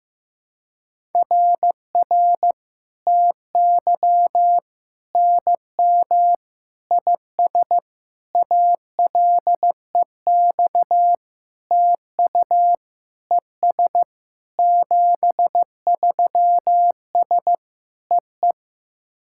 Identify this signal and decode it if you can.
Morse code, heard as a keyed tone, about 15 wpm, 700 Hz. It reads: RR TY NM IS ALEX TU ES 73S EE
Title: SOTA QSO Practice 15wpm